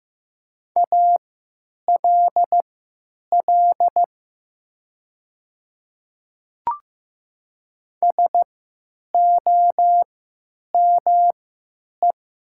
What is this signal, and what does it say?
Morse code, heard as a keyed tone, about 15 wpm, spaced as for 9 wpm, 700 Hz. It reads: ALL SOME
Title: 3x Character Spacing - Top 500 Words - 15wpm